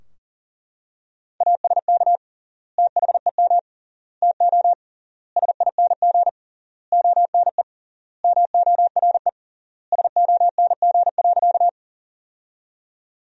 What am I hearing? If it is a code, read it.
Morse code, heard as a keyed tone, about 40 wpm, 700 Hz. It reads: ASK THEM TO SING ONE MORE SONG.
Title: 3x Word-Spacing - Sentences from Top 400 Words - 40wpm